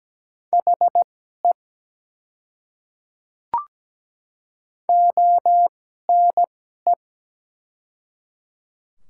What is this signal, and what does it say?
Morse code, heard as a keyed tone, about 17 wpm, spaced as for 12 wpm, 700 Hz. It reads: HE ONE
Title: 2x Character Spacing - Top 100 Words - 17wpm